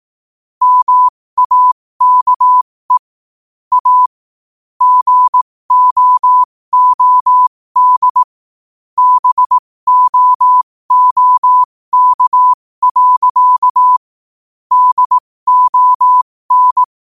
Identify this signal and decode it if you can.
Morse code, heard as a keyed tone, about 18 wpm, spaced as for 15 wpm, 1000 Hz. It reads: MAKE A GOOD BOOK. DON
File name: qotdfiles / 15wpm